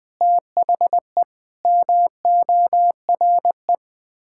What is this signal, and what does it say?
Morse code, heard as a keyed tone, about 20 wpm, 700 Hz. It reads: THE MORE
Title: Sets of 8 Words - 20wpm